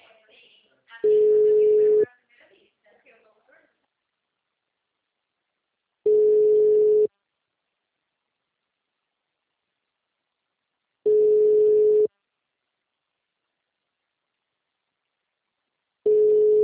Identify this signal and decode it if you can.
Ringback tone (Europe, 425 Hz), with speech.